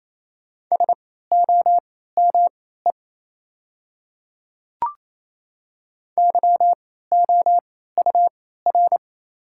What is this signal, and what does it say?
Morse code, heard as a keyed tone, about 28 wpm, spaced as for 16 wpm, 690 Hz. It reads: SOME YOUR
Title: 3x Character Spacing - Top 300 Words - 28wpm